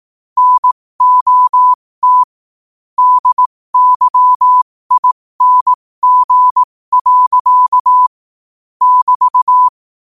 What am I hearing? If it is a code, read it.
Morse code, heard as a keyed tone, about 18 wpm, spaced as for 15 wpm, 1000 Hz. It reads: NOT DYING. <BT>
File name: qotdfiles / 15wpm